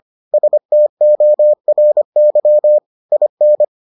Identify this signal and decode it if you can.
Morse code, heard as a keyed tone, about 25 wpm, 600 Hz. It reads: STORY IN